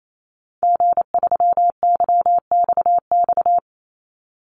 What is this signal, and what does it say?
Morse code, heard as a keyed tone, about 28 wpm, 700 Hz. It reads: G3YXX
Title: Call Signs - 28wpm